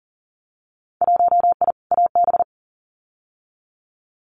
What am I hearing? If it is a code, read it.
Morse code, heard as a keyed tone, about 39 wpm, 700 Hz. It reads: 1I AB
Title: Winter Field Day Exchanges Encore 40wpm